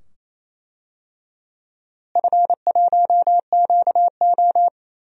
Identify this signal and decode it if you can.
Morse code, heard as a keyed tone, about 28 wpm, 700 Hz. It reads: F1QO
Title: DXCC-100 Call Signs - 4 Characters - Repeated 2x - 28wpm